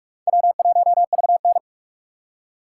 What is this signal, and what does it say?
Morse code, heard as a keyed tone, about 47 wpm, 700 Hz. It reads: W1VN
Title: USA Call Signs - 4 Characters - 4x Character Spacing - 45wpm